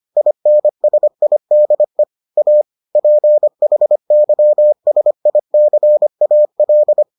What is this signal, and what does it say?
Morse code, heard as a keyed tone, about 25 wpm, 600 Hz. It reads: INSIDE A PHYSICAL